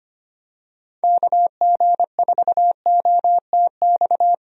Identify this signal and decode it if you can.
Morse code, heard as a keyed tone, about 25 wpm, 700 Hz. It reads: KG4OTX
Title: USA Call Signs - 6 Characters - Repeated 3x - 25wpm